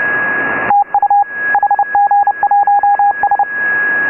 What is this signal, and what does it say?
Morse code, heard as a keyed tone, about 29 wpm, 810 Hz, with radio noise. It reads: TU HG1S